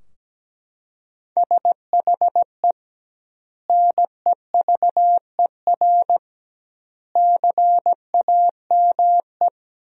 Morse code, heard as a keyed tone, about 17 wpm, 700 Hz. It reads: SHE NEVER CAME
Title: 2x Word-Spacing - Sentences from Top 300 Words - 17wpm